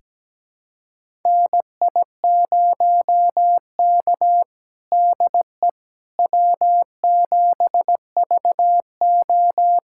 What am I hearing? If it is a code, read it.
Morse code, heard as a keyed tone, about 17 wpm, 700 Hz. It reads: NI0K DE W7VO
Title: USA Callsign DE Callsign - 4 Characters - 17wpm